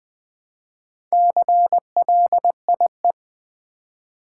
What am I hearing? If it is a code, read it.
Morse code, heard as a keyed tone, about 20 wpm, 700 Hz. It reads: CLIE
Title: Build A Word - 2 Letters At A Time - 6-Letter Words - 20wpm